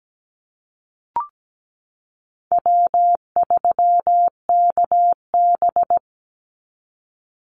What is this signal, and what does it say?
Morse code, heard as a keyed tone, about 17 wpm, 700 Hz. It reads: W3KB
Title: Easy Call Signs Repeated 2x - 17wpm